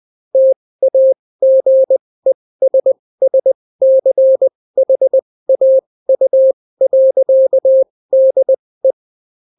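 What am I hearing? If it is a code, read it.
Morse code, heard as a keyed tone, about 20 wpm, spaced as for 16 wpm, 530 Hz. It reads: TAGESSCHAU.DE
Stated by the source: Daily Morse weather and top stories at 16 wpm, clean without QRN.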